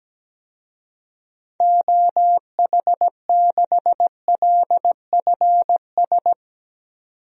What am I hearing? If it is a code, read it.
Morse code, heard as a keyed tone, about 17 wpm, 700 Hz. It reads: OH6LFS
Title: DXCC-100 Call Signs - 6 Characters - 17wpm